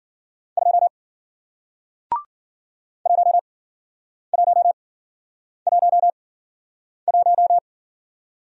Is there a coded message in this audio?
Yes: Morse code from 0.6 to 7.6 s, heard as a keyed tone, about 54 wpm, spaced as for 9 wpm, 700 Hz. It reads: 2 1111